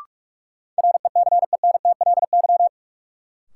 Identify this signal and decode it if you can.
Morse code, heard as a keyed tone, about 46 wpm, 700 Hz. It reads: RECENTLY
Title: Sentences from Top 1200 Words - New Word Drill Alt - 45wpm